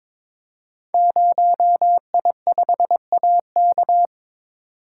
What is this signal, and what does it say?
Morse code, heard as a keyed tone, about 22 wpm, 700 Hz. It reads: 0I5AK
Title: Code Groups - 5 Letter:Numbers - 22wpm